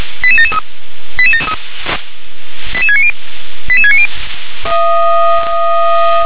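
Five-tone selective call (ZVEI), heard as quick runs of five tones, tones of about 70 ms. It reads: 88773 88690 88690
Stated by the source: Five-tone ZVEI selective call 88713, received by SDR.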